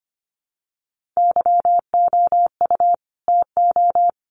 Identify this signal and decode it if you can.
Morse code, heard as a keyed tone, about 25 wpm, 700 Hz. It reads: YOU TO
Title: Sets of 2 Words 25wpm